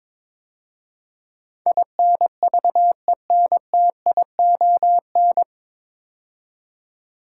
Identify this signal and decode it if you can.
Morse code, heard as a keyed tone, about 21 wpm, 700 Hz. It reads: INVENTION
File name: 9 Letter Words - First 3 Letters Given - 22wpm